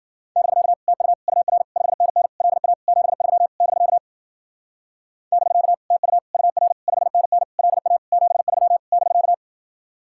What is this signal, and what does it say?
Morse code, heard as a keyed tone, about 50 wpm, 700 Hz. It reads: <BK> TU UR 5NN LA 73 <BK> <BK> TU UR 5NN LA 73 <BK>
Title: Speed-Racing Common QSO Phrases - 40wpm